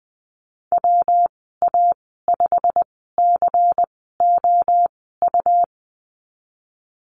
Morse code, heard as a keyed tone, about 20 wpm, spaced as for 15 wpm, 700 Hz. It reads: WA5COU
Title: 128 Lesson 21g - Number 6 - Call Signs 20wpm